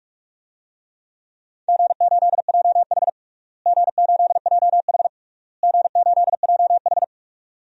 Morse code, heard as a keyed tone, about 46 wpm, 700 Hz. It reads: G8JH G8JH G8JH
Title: DXCC-100 Call Signs - 4 Characters - Repeated 3x - 45wpm